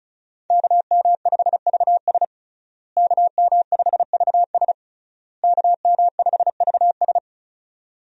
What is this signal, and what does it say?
Morse code, heard as a keyed tone, about 36 wpm, 700 Hz. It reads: KM5VS KM5VS KM5VS
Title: USA Call Signs - 5 Characters - Repeated 3x - 35wpm